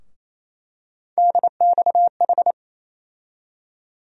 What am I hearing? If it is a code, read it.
Morse code, heard as a keyed tone, about 28 wpm, 700 Hz. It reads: DXH
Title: USA Call Sign Postfixes - 3 Letters 28wpm